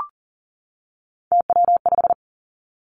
Morse code, heard as a keyed tone, about 42 wpm, 700 Hz. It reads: TW5
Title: Three Challenging Letter-Number Combinations 40wpm